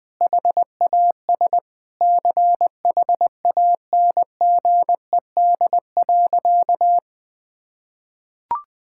Morse code, heard as a keyed tone, about 20 wpm, 700 Hz. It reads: HAS CHANGED.
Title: Sentences with 5-Word Phrases - Spoken Beginning - 20wpm